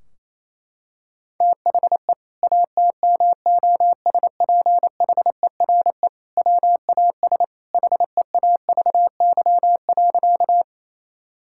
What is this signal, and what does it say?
Morse code, heard as a keyed tone, about 28 wpm, 700 Hz. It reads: THE ATMOSPHERE WAS HEAVY.
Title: Sentences from Top 1600 Words 28wpm